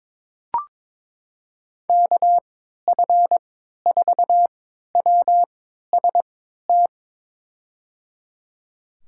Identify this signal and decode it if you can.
Morse code, heard as a keyed tone, about 22 wpm, spaced as for 12 wpm, 700 Hz. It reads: KF4WST
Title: USA Call Signs - 3x Character Spacing - 22wpm